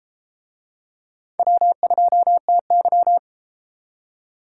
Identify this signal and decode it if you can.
Morse code, heard as a keyed tone, about 32 wpm, 700 Hz. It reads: W2TY